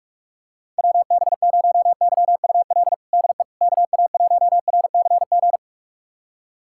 Morse code, heard as a keyed tone, about 44 wpm, 700 Hz. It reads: WB0YUL DE KA1RCG
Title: USA Callsign DE Callsign 45wpm